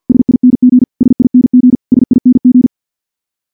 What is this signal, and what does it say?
Morse code, heard as a keyed tone, about 50 wpm, 250 Hz. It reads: SING SING SING
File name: morse / morse-training-50WPM